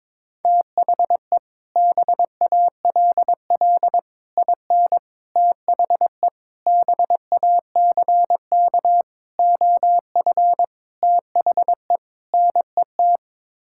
Morse code, heard as a keyed tone, about 22 wpm, 700 Hz. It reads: THE BALL IN THE BACK OF THE NET
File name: practice / 22wpm / Sets of 8 Words - 22wpm